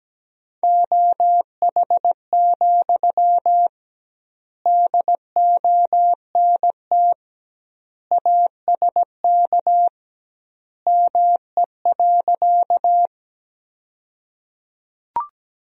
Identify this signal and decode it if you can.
Morse code, heard as a keyed tone, about 17 wpm, 700 Hz. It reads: OH, DONT ASK ME.
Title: 2x Word-Spacing - Sentences from Top 500 Words - 17wpm